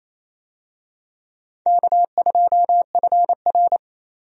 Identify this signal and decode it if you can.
Morse code, heard as a keyed tone, about 28 wpm, 700 Hz. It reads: K2FR
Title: USA Call Signs - 4 Characters - Repeated 2x - 28wpm